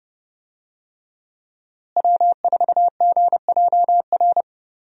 Morse code, heard as a keyed tone, about 30 wpm, 700 Hz. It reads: W4GJR
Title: USA Call Signs - 5 Characters - Repeated 3x - 30wpm